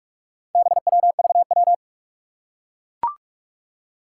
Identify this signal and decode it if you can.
Morse code, heard as a keyed tone, about 42 wpm, 700 Hz. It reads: BWVW
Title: Four Challenging Letter-Number Combinations 45wpm